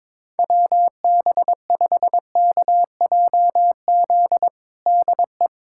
Morse code, heard as a keyed tone, about 22 wpm, 700 Hz. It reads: WB5KJZ DE
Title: USA Callsign DE Callsign - 6 Characters - 22wpm